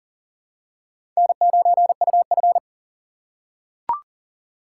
Morse code, heard as a keyed tone, about 37 wpm, 700 Hz. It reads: N9UF
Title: USA Call Signs - 4 Characters - 2x Character Spacing - 40wpm